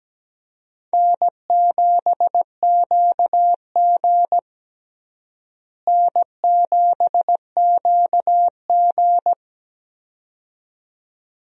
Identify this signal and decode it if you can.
Morse code, heard as a keyed tone, about 17 wpm, 700 Hz. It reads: N7QG N7QG
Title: USA Call Signs - 4 Characters - Repeated 2x - 17wpm